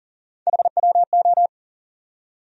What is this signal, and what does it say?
Morse code, heard as a keyed tone, about 38 wpm, 700 Hz. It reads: HWO
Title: USA Call Sign Postfixes - 3 Letters 40wpm